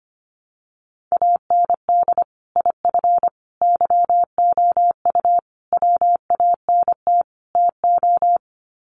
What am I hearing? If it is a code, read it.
Morse code, heard as a keyed tone, about 25 wpm, 700 Hz. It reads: AND IF YOU WANT TO
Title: Sets of 5 Words Encore 25wpm